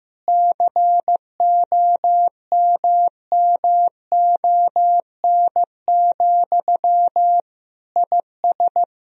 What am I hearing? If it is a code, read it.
Morse code, heard as a keyed tone, about 15 wpm, 700 Hz. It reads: COMMON, IS
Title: Sentences from Top 1000 Words - Review Part 3 - 15wpm